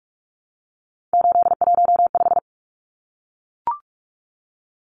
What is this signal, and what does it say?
Morse code, heard as a keyed tone, about 44 wpm, 700 Hz. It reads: 815
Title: Numbers - 3-Digit 45wpm